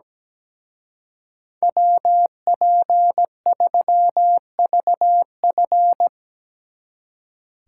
Morse code, heard as a keyed tone, about 19 wpm, 700 Hz. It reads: WP3VF
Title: Speed Racing - USA 2x2 Call Signs 17wpm